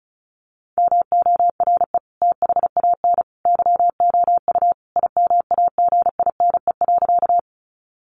Morse code, heard as a keyed tone, about 34 wpm, 700 Hz. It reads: MORE THAN YOU IMAGINE.
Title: Sentences from Top 1400 Words 35wpm